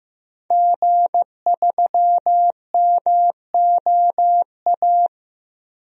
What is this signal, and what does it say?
Morse code, heard as a keyed tone, about 15 wpm, 700 Hz. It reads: G3MOA
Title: DXCC-100 Call Signs - 5 Characters - Repeated 3x - 15wpm